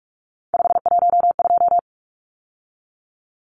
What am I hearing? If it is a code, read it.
Morse code, heard as a keyed tone, about 44 wpm, 700 Hz. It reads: 512
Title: Numbers - 3-Digit 45wpm